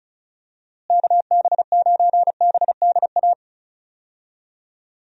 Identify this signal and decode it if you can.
Morse code, heard as a keyed tone, about 36 wpm, 700 Hz. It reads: KB9BDA
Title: USA Call Signs - 6 Characters - Repeated 2x - 35wpm